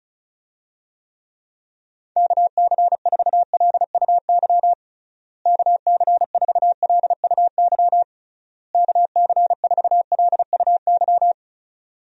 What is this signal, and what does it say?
Morse code, heard as a keyed tone, about 36 wpm, 700 Hz. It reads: KC4LUY KC4LUY KC4LUY
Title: USA Call Signs - 6 Characters - Repeated 3x - 35wpm